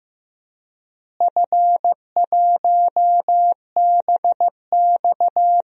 Morse code, heard as a keyed tone, about 15 wpm, 700 Hz. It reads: F1BX
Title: DXCC-100 Call Signs - 4 Characters - Repeated 2x - 15wpm